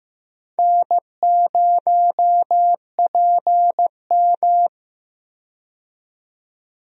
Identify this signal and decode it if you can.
Morse code, heard as a keyed tone, about 15 wpm, 700 Hz. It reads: N0PM
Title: USA Call Signs - 4 Characters - 15wpm